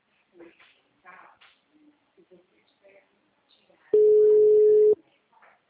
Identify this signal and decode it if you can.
Ringback tone (Europe, 425 Hz), with speech.